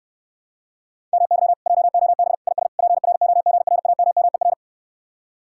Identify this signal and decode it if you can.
Morse code, heard as a keyed tone, about 69 wpm, 700 Hz. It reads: MY JOB IS PROGRAMMER